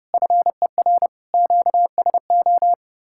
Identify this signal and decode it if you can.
Morse code, heard as a keyed tone, about 40 wpm, 700 Hz. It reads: FER QSO